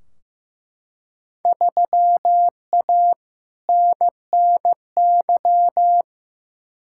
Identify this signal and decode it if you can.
Morse code, heard as a keyed tone, about 15 wpm, 700 Hz. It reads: 3A NNY
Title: Field Day Exchanges - 15wpm